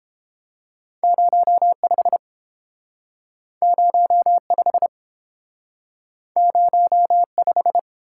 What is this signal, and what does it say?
Morse code, heard as a keyed tone, about 30 wpm, 700 Hz. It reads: 05 05 05